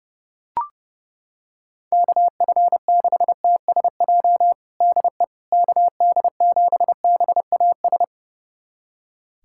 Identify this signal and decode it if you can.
Morse code, heard as a keyed tone, about 32 wpm, 700 Hz. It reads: KF6TSJ DE KD7BAS
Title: USA Callsign DE Callsign - 6 Characters - 30wpm